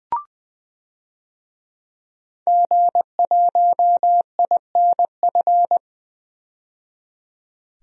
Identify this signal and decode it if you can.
Morse code, heard as a keyed tone, about 21 wpm, 700 Hz. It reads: G1INF
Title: DXCC-100 Call Signs - 5 Characters - 20wpm